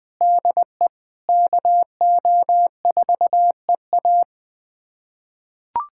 Morse code, heard as a keyed tone, about 21 wpm, 700 Hz. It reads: DE KO4EA
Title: USA Callsign DE Callsign - 5 Characters - 20wpm